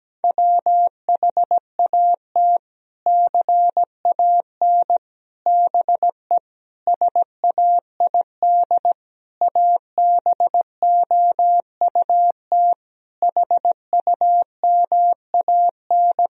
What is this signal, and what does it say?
Morse code, heard as a keyed tone, about 17 wpm, 700 Hz. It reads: WHAT CAN BE SAID ABOUT HUMAN
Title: Sentences from Top 1000 Words - Review Part 4 - 17wpm